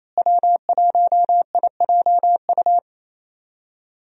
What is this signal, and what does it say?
Morse code, heard as a keyed tone, about 28 wpm, 700 Hz. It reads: W1IJU
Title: USA Call Signs - 5 Characters - Repeated 3x - 28wpm